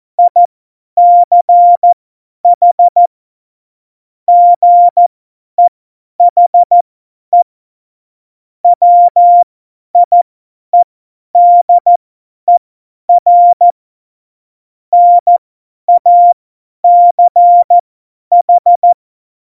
Morse code, heard as a keyed tone, about 14 wpm, spaced as for 10 wpm, 700 Hz. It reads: ICH GEHE WIEDER NACH
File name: Morsekurs / Witze / Witz 27 Juni 17, Tempo 50 F70